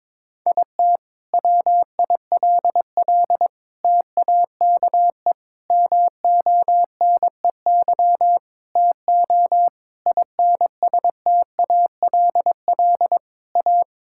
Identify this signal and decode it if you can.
Morse code, heard as a keyed tone, about 22 wpm, 700 Hz. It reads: IT WILL TAKE MONEY TO INSTALL A
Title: Sentences from Top 1600 Words 22wpm